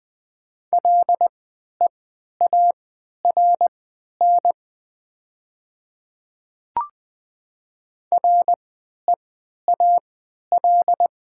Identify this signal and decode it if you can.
Morse code, heard as a keyed tone, about 20 wpm, spaced as for 11 wpm, 700 Hz. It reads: LEARN REAL
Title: 3x Character Spacing - Top 300 Words - 20wpm